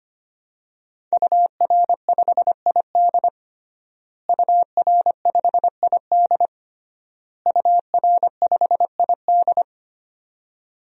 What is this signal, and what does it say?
Morse code, heard as a keyed tone, about 25 wpm, 700 Hz. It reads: UR5ID UR5ID UR5ID
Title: DXCC-100 Call Signs - 5 Characters - Repeated 3x - 25wpm